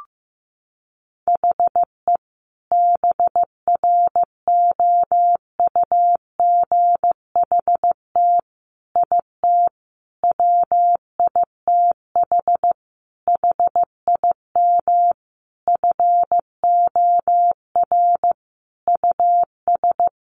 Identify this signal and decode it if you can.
Morse code, heard as a keyed tone, about 14 wpm, 700 Hz. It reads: HE BROUGHT IT WITH HIM FOR US
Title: Sentences from Top 500 Words 15wpm